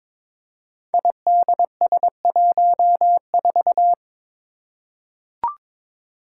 Morse code, heard as a keyed tone, about 22 wpm, 700 Hz. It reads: IDS14
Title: Five Challenging Letter-Number Combinations 22wpm